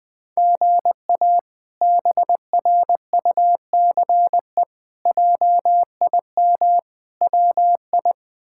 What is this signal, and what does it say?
Morse code, heard as a keyed tone, about 21 wpm, 700 Hz. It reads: GA BRUCE JIM WI
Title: K1USN SST QSO Practice - 20wpm